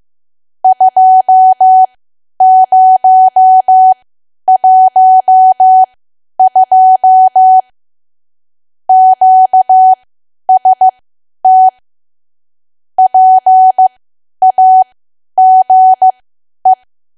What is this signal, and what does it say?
Morse code, heard as a keyed tone, about 15 wpm, spaced as for 10 wpm, 750 Hz. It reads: 2012 QST PAGE
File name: Archive / 10 WPM / 10WPM